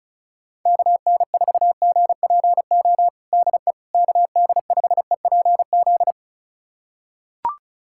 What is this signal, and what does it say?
Morse code, heard as a keyed tone, about 34 wpm, 700 Hz. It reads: KN4GPO DE KD5EPZ
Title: USA Callsign DE Callsign - 6 Characters - 35wpm